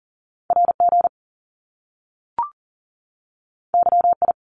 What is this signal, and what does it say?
Morse code, heard as a keyed tone, about 39 wpm, 700 Hz. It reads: RG YI
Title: Top Two Letter Combinations in Words 40wpm